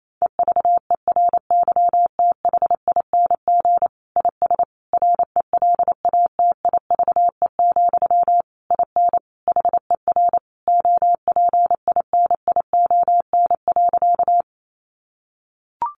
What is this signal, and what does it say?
Morse code, heard as a keyed tone, about 28 wpm, 700 Hz. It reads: EVERYTHING IS RELATIVE, IN HER OPINION.
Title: Sentences from Top 1400 Words 28wpm